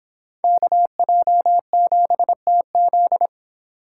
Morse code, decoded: KJ7TZ

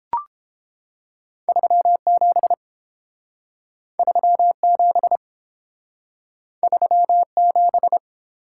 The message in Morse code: 37 37 37

